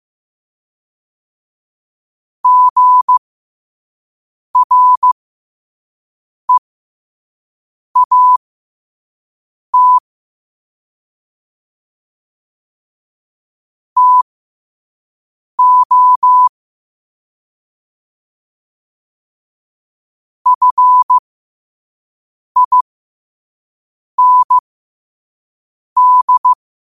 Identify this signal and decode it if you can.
Morse code, heard as a keyed tone, about 15 wpm, spaced as for 5 wpm, 1000 Hz. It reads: GREAT TO FIND
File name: qotdfiles / 5wpm